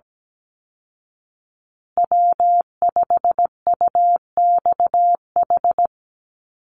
Morse code, heard as a keyed tone, about 17 wpm, 700 Hz. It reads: W5UXH
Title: Call Signs Repeated 2x - 17wpm